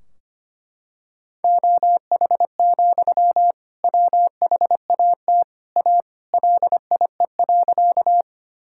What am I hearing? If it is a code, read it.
Morse code, heard as a keyed tone, about 25 wpm, 700 Hz. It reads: OH, WHAT A LIE.
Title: Sentences from Top 1600 Words 25wpm